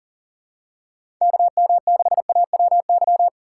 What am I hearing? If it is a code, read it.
Morse code, heard as a keyed tone, about 40 wpm, 700 Hz. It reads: KM6AWY
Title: USA Call Signs - 6 Characters - 3x Character Spacing - 40wpm